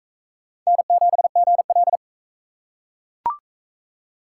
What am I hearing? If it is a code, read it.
Morse code, heard as a keyed tone, about 41 wpm, 700 Hz. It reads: N7GL